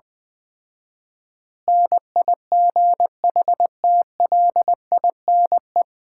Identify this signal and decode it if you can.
Morse code, heard as a keyed tone, about 20 wpm, 700 Hz. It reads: NIGHTLINE
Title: 4x Word-Spacing - Sets of 5 Words - 20wpm